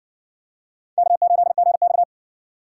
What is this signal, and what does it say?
Morse code, heard as a keyed tone, about 60 wpm, 700 Hz. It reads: K8GX